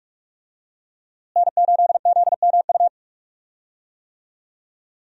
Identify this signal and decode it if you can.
Morse code, heard as a keyed tone, about 46 wpm, 700 Hz. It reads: N8ZMU
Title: USA Call Signs - 5 Characters - Repeated 3x - 45wpm